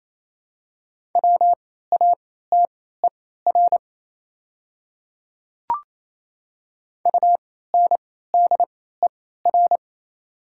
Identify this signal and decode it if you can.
Morse code, heard as a keyed tone, about 27 wpm, spaced as for 16 wpm, 690 Hz. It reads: WATER UNDER
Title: 3x Character Spacing - Top 300 Words - 28wpm